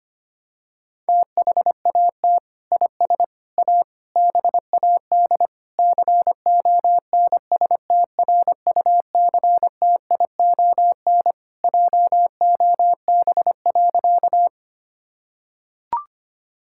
Morse code, heard as a keyed tone, about 25 wpm, 700 Hz. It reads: THAT IS A BAD CONSTRUCTION JOB.
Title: Sentences from Top 1600 Words 25wpm